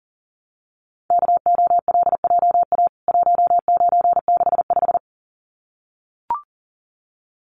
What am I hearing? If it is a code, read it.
Morse code, heard as a keyed tone, about 40 wpm, 700 Hz. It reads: KOLJA 1965
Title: CWT Exchange 40wpm